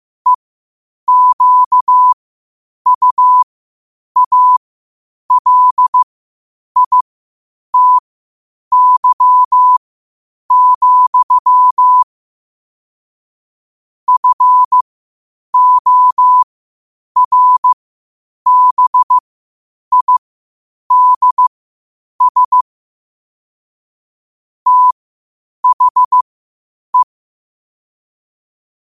Morse code, heard as a keyed tone, about 15 wpm, spaced as for 8 wpm, 1000 Hz. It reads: EQUALITY, FORBIDS THE
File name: qotdfiles / 8wpm